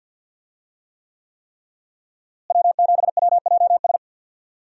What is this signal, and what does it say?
Morse code, heard as a keyed tone, about 51 wpm, 700 Hz. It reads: W7WJS
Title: USA Call Signs - 5 Characters - Repeated 3x - 50wpm